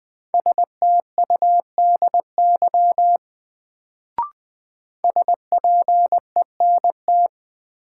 Morse code, heard as a keyed tone, about 20 wpm, 700 Hz. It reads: STUDY SPENT
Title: Sets of 5 Words - New Word Drill - 20wpm